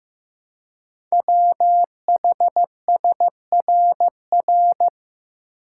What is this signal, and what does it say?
Morse code, heard as a keyed tone, about 15 wpm, 700 Hz. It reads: WHSRR
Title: Code Groups - 5 Letters - 15wpm